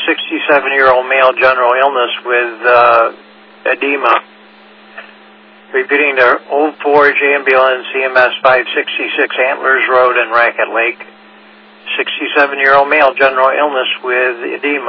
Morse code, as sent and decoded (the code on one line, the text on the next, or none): none